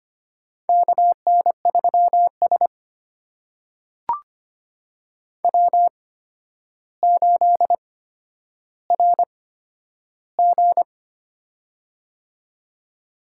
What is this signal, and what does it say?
Morse code, heard as a keyed tone, about 25 wpm, spaced as for 11 wpm, 700 Hz. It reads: KN3S W8RG
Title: USA Call Signs - 4 Characters - 8x Character Spacing - 25wpm